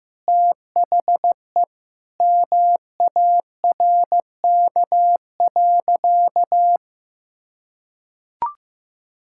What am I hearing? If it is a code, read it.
Morse code, heard as a keyed tone, about 16 wpm, 700 Hz. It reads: THE MARK.
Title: Sentences from Top 1000 Words - Review Part 6 - 15wpm